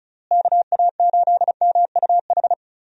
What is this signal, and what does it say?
Morse code, heard as a keyed tone, about 34 wpm, 700 Hz. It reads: KA8MUH
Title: USA Call Signs - 6 Characters - Repeated 3x - 35wpm